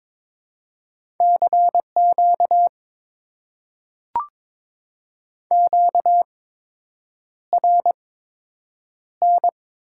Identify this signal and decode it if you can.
Morse code, heard as a keyed tone, about 22 wpm, spaced as for 8 wpm, 700 Hz. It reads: CQ QRN
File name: Top 90 QSO Elements - 8x Character Spacing - 22wpm